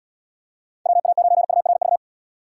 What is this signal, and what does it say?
Morse code, heard as a keyed tone, about 78 wpm, 700 Hz. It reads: WA9URV